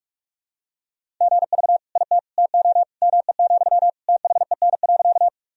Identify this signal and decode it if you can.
Morse code, heard as a keyed tone, about 46 wpm, 700 Hz. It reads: GV IT TO ME, THEN.
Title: Sentences from Top 200 Words and Abbreviations 45wpm